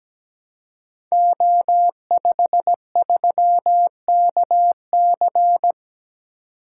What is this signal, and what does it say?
Morse code, heard as a keyed tone, about 17 wpm, 700 Hz. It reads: O53KC
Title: Code Groups - 5 Letter:Numbers - 17wpm